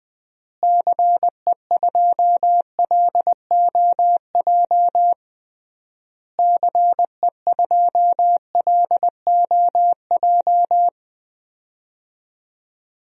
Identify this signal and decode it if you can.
Morse code, heard as a keyed tone, about 20 wpm, 700 Hz. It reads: CE2LOJ CE2LOJ
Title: DXCC-100 Call Signs - 6 Characters - Repeated 3x - 20wpm